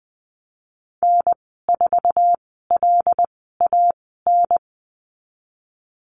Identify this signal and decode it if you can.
Morse code, heard as a keyed tone, about 20 wpm, spaced as for 15 wpm, 700 Hz. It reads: N4LAN
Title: 128 Lesson 21g - Number 6 - Call Signs 20wpm